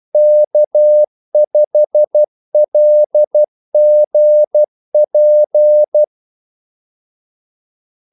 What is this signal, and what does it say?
Morse code, heard as a keyed tone, about 12 wpm, 600 Hz. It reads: K5LGP
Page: 2 ekstra ordmellemrum mellem kaldesignalerne